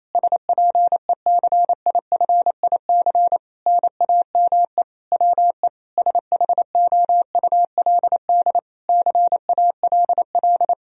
Morse code, heard as a keyed tone, about 28 wpm, 700 Hz. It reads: SPECIFIC NAME WE SHOULD CALL
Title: Sentences from Top 1000 Words - Review Part 3 - 28wpm